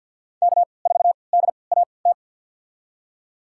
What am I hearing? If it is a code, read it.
Morse code, heard as a keyed tone, about 52 wpm, spaced as for 29 wpm, 700 Hz. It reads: K4DAT
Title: USA Call Signs - 5 Characters - 3x Character Spacing - 50wpm